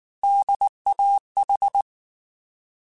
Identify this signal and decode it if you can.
Morse code, heard as a keyed tone, about 19 wpm, 800 Hz. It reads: DAH
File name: Alphabetical / Alphabetical mp3s / Ds @18